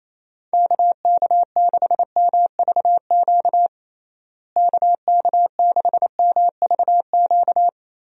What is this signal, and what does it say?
Morse code, heard as a keyed tone, about 27 wpm, 700 Hz. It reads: KK6MVQ KK6MVQ
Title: USA Call Signs - 6 Characters - Repeated 3x - 28wpm